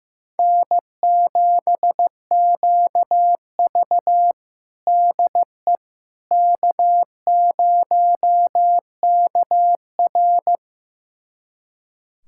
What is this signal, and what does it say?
Morse code, heard as a keyed tone, about 15 wpm, 700 Hz. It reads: N7QV DE K0KR
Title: USA Callsign DE Callsign - 4 Characters - 15wpm